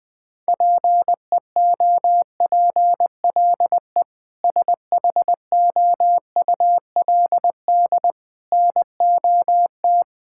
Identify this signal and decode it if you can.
Morse code, heard as a keyed tone, about 20 wpm, 700 Hz. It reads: PEOPLE SHOULD NOT
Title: Sentences with 8-Word Phrases - 20wpm